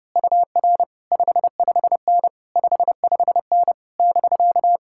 Morse code, heard as a keyed tone, about 28 wpm, 700 Hz. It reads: UR 55N 55N <BK>
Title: Speed-Racing Common QSO Phrases - 30wpm